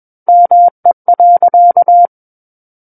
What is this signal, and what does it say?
Morse code, heard as a keyed tone, about 21 wpm, 700 Hz. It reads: ME.
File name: Book One / 12-Chapter 12 - 21 WPM